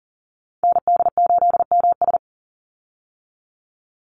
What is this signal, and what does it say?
Morse code, heard as a keyed tone, about 40 wpm, 700 Hz. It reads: ND8MS
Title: Call Signs Repeated 3x - 40wpm